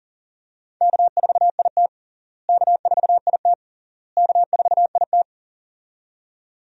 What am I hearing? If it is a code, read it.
Morse code, heard as a keyed tone, about 40 wpm, 700 Hz. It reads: K4IT K4IT K4IT
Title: USA Call Signs - 4 Characters - Repeated 3x - 40wpm